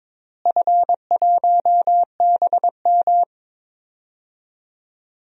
Morse code, heard as a keyed tone, about 22 wpm, 700 Hz. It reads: F1BM
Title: DXCC-100 Call Signs - 4 Characters - Repeated 2x - 22wpm